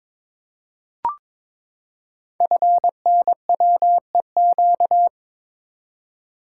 Morse code, heard as a keyed tone, about 22 wpm, 700 Hz. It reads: FNWEQ
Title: Code Groups - 5 Letters - 22wpm